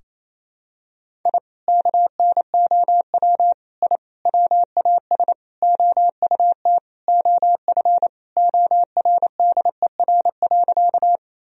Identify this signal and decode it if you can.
Morse code, heard as a keyed tone, about 27 wpm, 700 Hz. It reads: I KNOW I WAS OUT OF ORDER.
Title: Sentences from Top 1000 Words - Review Part 2 - 28wpm